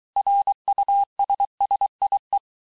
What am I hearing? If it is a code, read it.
Morse code, heard as a keyed tone, about 23 wpm, 800 Hz. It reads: RUSSIE
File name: texte-clair-vitesse-17mpm-puis-22mpm